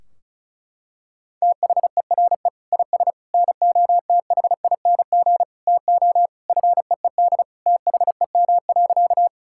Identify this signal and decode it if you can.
Morse code, heard as a keyed tone, about 34 wpm, 700 Hz. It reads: THERE IS NOTHING TO FEED THEM.
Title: Sentences from Top 1600 Words 35wpm